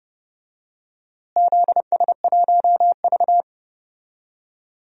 Morse code, heard as a keyed tone, about 30 wpm, 700 Hz. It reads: ZS1V
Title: DXCC-100 Call Signs - 4 Characters - Repeated 3x - 30wpm